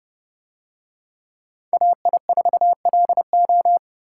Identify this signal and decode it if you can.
Morse code, heard as a keyed tone, about 29 wpm, 700 Hz. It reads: AI4LO